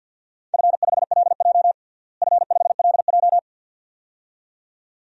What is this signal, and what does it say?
Morse code, heard as a keyed tone, about 52 wpm, 700 Hz. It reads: F5LJ F5LJ
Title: DXCC-100 Call Signs - 4 Characters - Repeated 2x - 50wpm